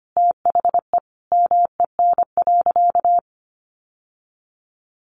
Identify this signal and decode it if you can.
Morse code, heard as a keyed tone, about 25 wpm, 700 Hz. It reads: THE MEN.
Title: Sentences from Top 1400 Words 25wpm